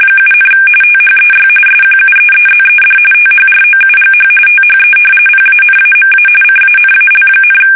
RTTY, about 45.45 baud. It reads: 7, THEY SQUEEZED THROUGH THE PANAMA CANAL,
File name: fsk850 / battleshipnj